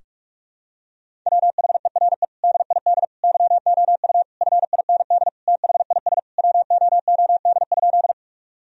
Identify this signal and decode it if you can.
Morse code, heard as a keyed tone, about 41 wpm, 700 Hz. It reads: WHERE DID YOU FIND THIS WOOD?
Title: Sentences from Top 1000 Words - Review Part 2 - 45wpm